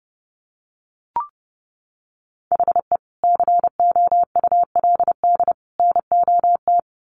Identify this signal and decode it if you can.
Morse code, heard as a keyed tone, about 30 wpm, 700 Hz. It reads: HE COULD NOT
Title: Sets of 3 Words 30wpm